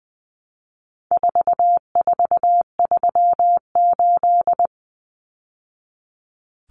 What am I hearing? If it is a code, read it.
Morse code, heard as a keyed tone, about 20 wpm, 700 Hz. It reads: 4438
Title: Numbers - 4-Digit 20wpm